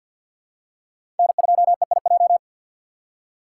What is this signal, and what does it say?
Morse code, heard as a keyed tone, about 51 wpm, 700 Hz. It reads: N2EIJ
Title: USA Call Signs - 5 Characters - 3x Character Spacing - 50wpm